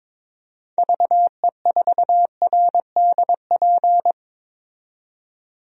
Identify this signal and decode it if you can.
Morse code, heard as a keyed tone, about 22 wpm, 700 Hz. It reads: VE4RDP
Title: DXCC-100 Call Signs - 6 Characters - 22wpm